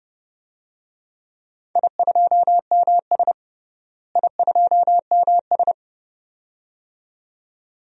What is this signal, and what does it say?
Morse code, heard as a keyed tone, about 30 wpm, 700 Hz. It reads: I2MS I2MS
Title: DXCC-100 Call Signs - 4 Characters - Repeated 2x - 30wpm